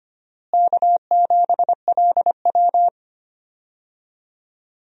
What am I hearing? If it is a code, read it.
Morse code, heard as a keyed tone, about 25 wpm, 700 Hz. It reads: K7LW
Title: USA Call Signs - 4 Characters - Repeated 3x - 25wpm